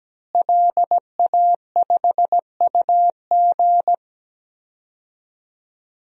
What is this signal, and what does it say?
Morse code, heard as a keyed tone, about 17 wpm, 700 Hz. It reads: LA5UG
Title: DXCC-100 Call Signs - 5 Characters - Repeated 2x - 17wpm